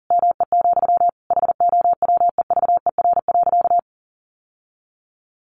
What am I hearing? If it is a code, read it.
Morse code, heard as a keyed tone, about 40 wpm, 700 Hz. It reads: ME, HOWEVER.
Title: Sentences from Top 600 Words 40wpm